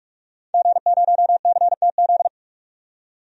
Morse code, heard as a keyed tone, about 46 wpm, 700 Hz. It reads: G0CTZ
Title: DXCC-100 Call Signs - 5 Characters - 45wpm